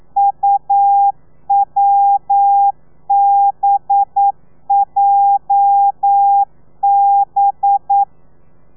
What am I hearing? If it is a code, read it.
Morse code, heard as a keyed tone, about 9 wpm, 800 Hz. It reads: UWBJB
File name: telegrafia / i2tfj / Lez8 40